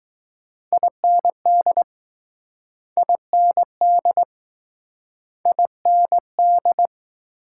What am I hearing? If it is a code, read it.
Morse code, heard as a keyed tone, about 20 wpm, 700 Hz. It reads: IND IND IND